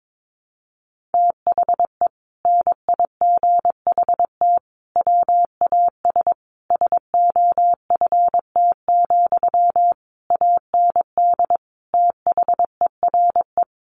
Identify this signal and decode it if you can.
Morse code, heard as a keyed tone, about 22 wpm, 700 Hz. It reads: THE NIGHT WAS SOFT, AND THERE
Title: Sentences from Top 1500 Words 22wpm